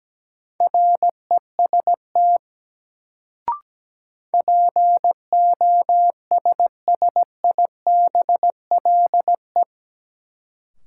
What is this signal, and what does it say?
Morse code, heard as a keyed tone, about 17 wpm, 700 Hz. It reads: REST POSSIBLE
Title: Sentences from Top 500 Words - New Word Drill - 17wpm